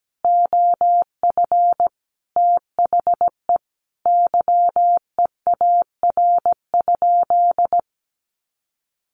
Morse code, heard as a keyed tone, about 17 wpm, 700 Hz. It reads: OF THE YEAR?
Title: Sentences from Top 1400 Words 17wpm